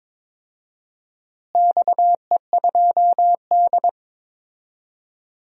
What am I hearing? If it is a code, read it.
Morse code, heard as a keyed tone, about 22 wpm, 700 Hz. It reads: XE2D
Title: DXCC-100 Call Signs - 4 Characters - Repeated 3x - 22wpm